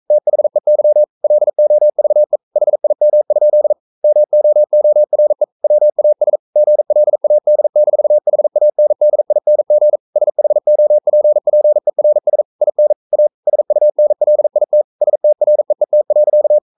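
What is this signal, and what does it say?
Morse code, heard as a keyed tone, about 42 wpm, 600 Hz. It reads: THEY LOVE HIM? MOORE WAS GLAD-HANDING SHOPPERS IN A SUNLIT STREET.
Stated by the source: Morse code practice at 41 wpm, from CW med Gnister.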